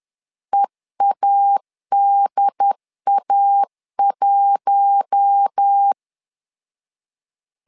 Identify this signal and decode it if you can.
Morse code, heard as a keyed tone, about 10 wpm, 800 Hz. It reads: EADA1
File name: LEZIONI / 4 Q1 10wpm 2448